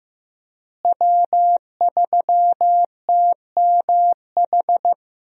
Morse code, heard as a keyed tone, about 15 wpm, 700 Hz. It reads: W3TMH
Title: USA Call Signs - 5 Characters - Repeated 2x - 15wpm